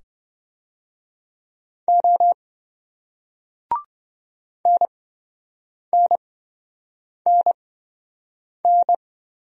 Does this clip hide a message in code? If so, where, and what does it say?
Yes: Morse code from 1.9 to 9.0 s, heard as a keyed tone, about 27 wpm, spaced as for 7 wpm, 700 Hz. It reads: O NNNN